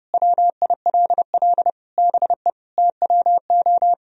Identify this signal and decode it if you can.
Morse code, heard as a keyed tone, about 30 wpm, 700 Hz. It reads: WILL BE TWO
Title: Sentences from Top 1000 Words - Review Part 3 - 30wpm